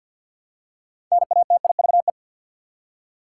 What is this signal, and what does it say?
Morse code, heard as a keyed tone, about 53 wpm, 700 Hz. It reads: NATIVE
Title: Six Letter Words - 50wpm